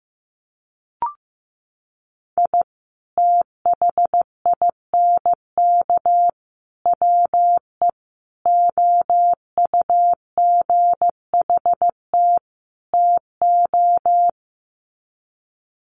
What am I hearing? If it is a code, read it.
Morse code, heard as a keyed tone, about 15 wpm, 700 Hz. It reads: I THINK WE OUGHT TO
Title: Sets of 5 Words Encore 15wpm